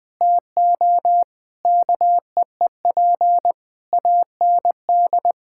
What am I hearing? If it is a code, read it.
Morse code, heard as a keyed tone, about 20 wpm, 700 Hz. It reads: TO KEEP AND
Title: Sets of 9 Words - 20wpm